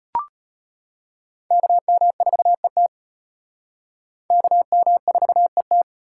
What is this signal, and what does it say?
Morse code, heard as a keyed tone, about 34 wpm, 700 Hz. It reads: KM4ET KM4ET